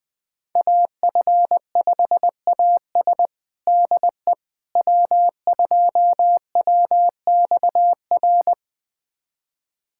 Morse code, heard as a keyed tone, about 20 wpm, 700 Hz. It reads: AF5AS DE W2WXR
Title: USA Callsign DE Callsign - 5 Characters - 20wpm